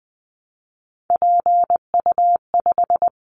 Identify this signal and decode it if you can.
Morse code, heard as a keyed tone, about 20 wpm, 700 Hz. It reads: PU5
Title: Callsign Prefixes for Top 50 DXCC - 20wpm